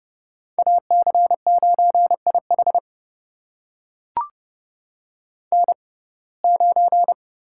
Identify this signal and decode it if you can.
Morse code, heard as a keyed tone, about 30 wpm, spaced as for 22 wpm, 700 Hz. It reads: AC9IH N9
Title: USA Call Signs - 6x Character Spacing - 30wpm